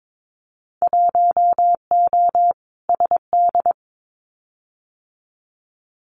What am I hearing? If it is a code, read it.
Morse code, heard as a keyed tone, about 22 wpm, 700 Hz. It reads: 1O SD
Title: Winter Field Day Exchanges Encore 22wpm